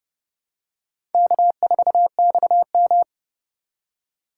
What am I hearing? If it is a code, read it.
Morse code, heard as a keyed tone, about 29 wpm, 700 Hz. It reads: K4XM
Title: USA Call Signs - 4x Character Spacing - 30wpm